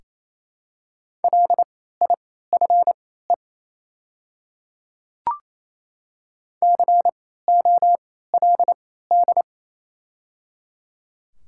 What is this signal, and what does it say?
Morse code, heard as a keyed tone, about 28 wpm, spaced as for 16 wpm, 700 Hz. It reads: LIFE COLD
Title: 3x Character Spacing - Top 500 Words - 28wpm